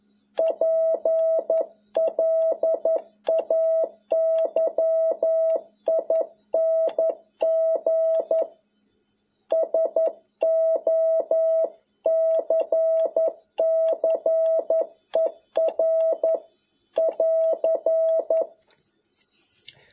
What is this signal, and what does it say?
Morse code, heard as a keyed tone, about 11 wpm, 650 Hz. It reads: PLAYING SOCCER <AR>
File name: jh8cbh / 2021cw